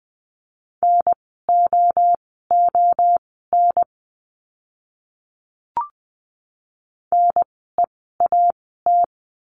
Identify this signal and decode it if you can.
Morse code, heard as a keyed tone, about 20 wpm, spaced as for 14 wpm, 700 Hz. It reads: NOON NEAT
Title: 054 Lesson 10f - Letter r - Words 20wpm